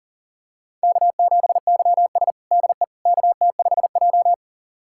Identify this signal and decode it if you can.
Morse code, heard as a keyed tone, about 40 wpm, 700 Hz. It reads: K7YS DE KT5J
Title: USA Callsign DE Callsign - 4 Characters - 40wpm